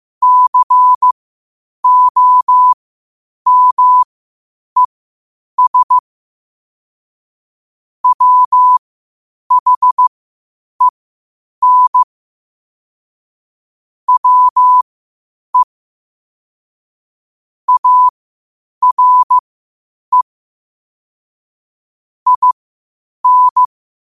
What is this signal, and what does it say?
Morse code, heard as a keyed tone, about 15 wpm, spaced as for 8 wpm, 1000 Hz. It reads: COMES WHEN WE ARE IN
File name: qotdfiles / 8wpm